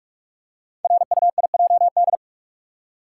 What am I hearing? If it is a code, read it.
Morse code, heard as a keyed tone, about 44 wpm, 700 Hz. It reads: RUIJD
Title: Code Groups - 5 Letters - 45wpm